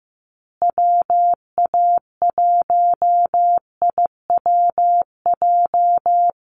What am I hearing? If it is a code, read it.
Morse code, heard as a keyed tone, about 15 wpm, 700 Hz. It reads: WA1IWJ
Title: USA Call Signs Repeated 2x 15wpm